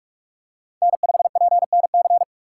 Morse code, heard as a keyed tone, about 46 wpm, 700 Hz. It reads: N5PNC
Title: USA Call Signs - 5 Characters - 3x Character Spacing - 45wpm